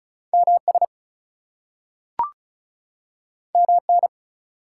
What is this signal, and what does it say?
Morse code, heard as a keyed tone, about 36 wpm, 700 Hz. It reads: MS MN
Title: US States and Canadian Provinces 35wpm